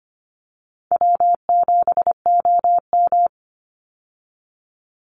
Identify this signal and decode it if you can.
Morse code, heard as a keyed tone, about 25 wpm, 700 Hz. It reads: W7OM
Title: Easy Call Signs Repeated 2x - 25wpm